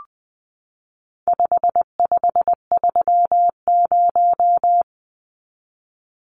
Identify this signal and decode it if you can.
Morse code, heard as a keyed tone, about 20 wpm, 700 Hz. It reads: 5530
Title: Numbers - 4-Digit 20wpm